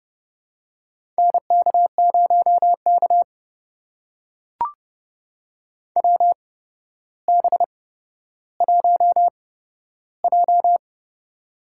Morse code, heard as a keyed tone, about 30 wpm, spaced as for 13 wpm, 700 Hz. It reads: NK0K WB1J
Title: USA Call Signs - 4 Characters - 8x Character Spacing - 30wpm